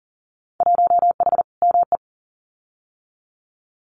Morse code, heard as a keyed tone, about 40 wpm, 700 Hz. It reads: 1H ME
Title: Winter Field Day Exchanges 40wpm